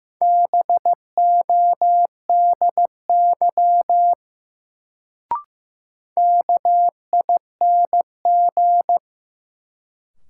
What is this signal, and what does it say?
Morse code, heard as a keyed tone, about 15 wpm, 700 Hz. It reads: BODY KING
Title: Sentences from Top 400 Words - New Word Drill - 15wpm